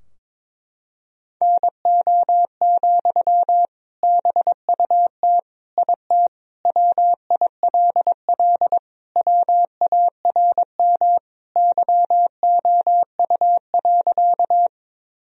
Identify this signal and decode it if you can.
Morse code, heard as a keyed tone, about 21 wpm, 700 Hz. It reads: NO, BUT IT WILL WARM YOU.
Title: Sentences from Top 1000 Words - Review Part 2 - 22wpm